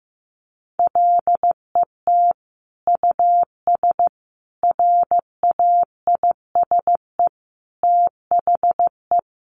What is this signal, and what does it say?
Morse code, heard as a keyed tone, about 15 wpm, 700 Hz. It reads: LET US RAISE THE
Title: Sentences from Top 1400 Words 15wpm